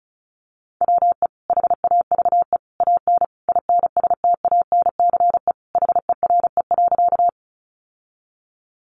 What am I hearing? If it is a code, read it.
Morse code, heard as a keyed tone, about 34 wpm, 700 Hz. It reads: WE HAVE AN INSTANCE HERE.
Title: Sentences from Top 800 Words 35wpm